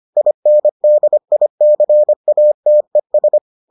Morse code, heard as a keyed tone, about 25 wpm, 600 Hz. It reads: INDICATES